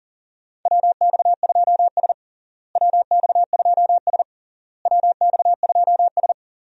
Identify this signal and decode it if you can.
Morse code, heard as a keyed tone, about 37 wpm, 700 Hz. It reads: WX2S WX2S WX2S
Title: USA Call Signs - 4 Characters - Repeated 3x - 40wpm